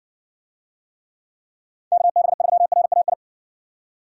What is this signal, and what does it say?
Morse code, heard as a keyed tone, about 54 wpm, 700 Hz. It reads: KB3RAI